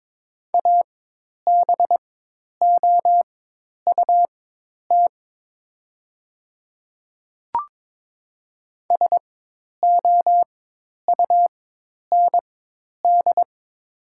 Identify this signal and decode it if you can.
Morse code, heard as a keyed tone, about 22 wpm, spaced as for 10 wpm, 700 Hz. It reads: ABOUT SOUND
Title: 4x Character Spacing - Top 200 Words - 22wpm